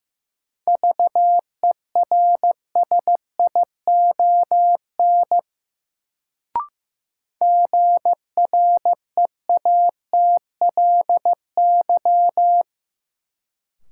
Morse code, heard as a keyed tone, about 15 wpm, 700 Hz. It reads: VERSION GREATLY
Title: Sentences from Top 1400 Words - New Word Drill Alt - 15wpm